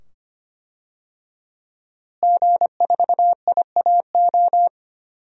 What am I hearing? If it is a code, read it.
Morse code, heard as a keyed tone, about 25 wpm, 700 Hz. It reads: G4IAO
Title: DXCC-100 Call Signs - 5 Characters - Repeated 2x - 25wpm